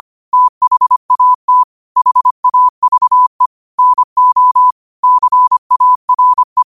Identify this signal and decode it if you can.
Morse code, heard as a keyed tone, about 25 wpm, 1000 Hz. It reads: THAT HAVE NO CARE